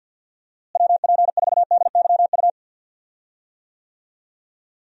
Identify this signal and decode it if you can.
Morse code, heard as a keyed tone, about 47 wpm, 700 Hz. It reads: WP4DYU
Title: USA Call Signs - 6 Characters - Repeated 2x - 50wpm